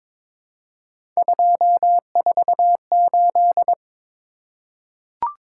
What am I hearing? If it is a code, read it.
Morse code, heard as a keyed tone, about 21 wpm, 700 Hz. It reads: 248
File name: practice / 22wpm / Three Letter-Number Combinations - 22wpm